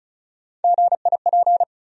Morse code, heard as a keyed tone, about 36 wpm, 700 Hz. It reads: GIP
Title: USA Call Sign Postfixes - 3 Letters 35wpm